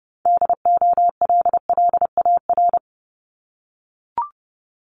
Morse code, heard as a keyed tone, about 30 wpm, 700 Hz. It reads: DOLLAR
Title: Multi-letter Phonemes 30wpm